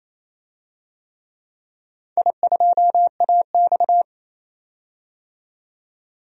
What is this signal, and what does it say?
Morse code, heard as a keyed tone, about 27 wpm, 700 Hz. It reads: I2AX
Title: DXCC-100 Call Signs - 4 Characters - 28wpm